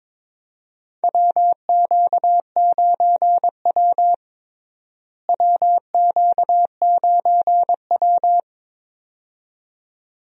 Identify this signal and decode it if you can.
Morse code, heard as a keyed tone, about 22 wpm, 700 Hz. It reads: WQ9W WQ9W
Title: USA Call Signs - 4 Characters - Repeated 2x - 22wpm